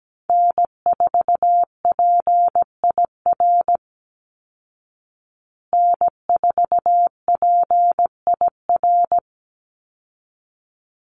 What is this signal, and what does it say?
Morse code, heard as a keyed tone, about 17 wpm, 700 Hz. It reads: N4PIR N4PIR
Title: Call Signs Repeated 3x - 17wpm